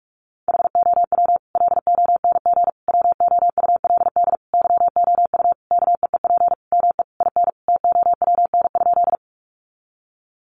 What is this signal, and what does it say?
Morse code, heard as a keyed tone, about 44 wpm, 700 Hz. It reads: HOW LONG WOULD YOU KEEP ME IN TOWN?
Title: Sentences from Top 500 Words 45wpm